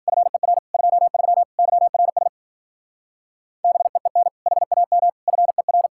Morse code, heard as a keyed tone, about 49 wpm, 700 Hz. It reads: FER 23 YRS BEEN HAM FER
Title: Speed-Racing Common QSO Phrases - 40wpm